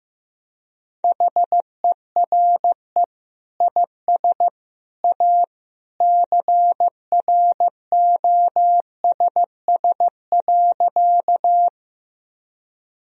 Morse code, heard as a keyed tone, about 15 wpm, 700 Hz. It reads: HERE IS A CROSS.
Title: Sentences from Top 1000 Words - Review Part 1 - 15wpm